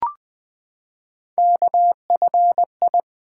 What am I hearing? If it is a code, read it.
Morse code, heard as a keyed tone, about 20 wpm, 700 Hz. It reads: KFI
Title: Three Letter-Number Combinations - 20wpm